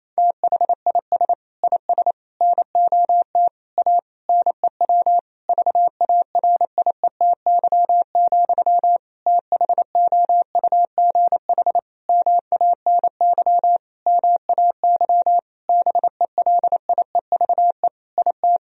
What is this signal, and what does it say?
Morse code, heard as a keyed tone, about 28 wpm, 700 Hz. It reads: THIS IS NOT A NEW VARIETY, THOUGH MANY MAY BELIEVE IT
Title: Sentences from Top 1000 Words - Review Part 4 - 28wpm